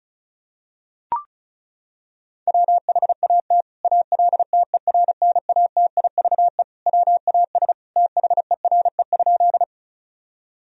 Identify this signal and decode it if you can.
Morse code, heard as a keyed tone, about 36 wpm, 700 Hz. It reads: WHAT ALTERNATIVE WAS THERE?
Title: Sentences from Top 1000 Words - Review Part 4 - 35wpm